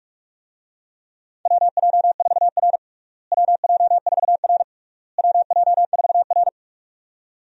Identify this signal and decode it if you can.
Morse code, heard as a keyed tone, about 46 wpm, 700 Hz. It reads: WJ4R WJ4R WJ4R
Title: USA Call Signs - 4 Characters - Repeated 3x - 45wpm